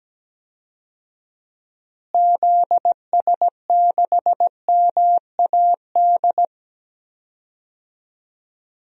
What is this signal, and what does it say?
Morse code, heard as a keyed tone, about 17 wpm, 700 Hz. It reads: ZS6MAD
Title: DXCC-100 Call Signs - 6 Characters - Repeated 2x - 17wpm